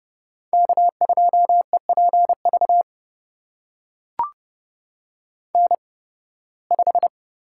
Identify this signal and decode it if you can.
Morse code, heard as a keyed tone, about 30 wpm, spaced as for 20 wpm, 700 Hz. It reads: K2EPV N5
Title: USA Call Signs - 8x Character Spacing - 30wpm